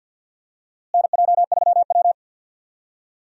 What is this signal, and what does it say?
Morse code, heard as a keyed tone, about 51 wpm, 700 Hz. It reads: NJ3W
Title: USA Call Signs - 4 Characters - Repeated 2x - 50wpm